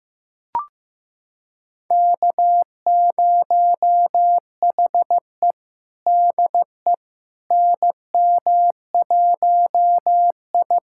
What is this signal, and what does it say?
Morse code, heard as a keyed tone, about 15 wpm, 700 Hz. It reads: K0HE DE NM1I
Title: USA Callsign DE Callsign - 4 Characters - 15wpm